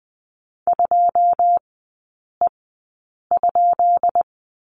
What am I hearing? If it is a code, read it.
Morse code, heard as a keyed tone, about 20 wpm, spaced as for 8 wpm, 690 Hz. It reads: 2E?
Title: 212 Lesson 34e - Number 8 - Three Characters 20wpm